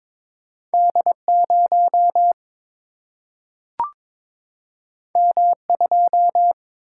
Morse code, heard as a keyed tone, about 22 wpm, 700 Hz. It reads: D 0 M 2
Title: Single-Letter Single-Number 22wpm